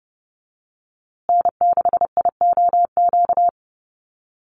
Morse code, heard as a keyed tone, about 30 wpm, 700 Hz. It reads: N6IOQ
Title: USA Call Signs Repeated 3x 30wpm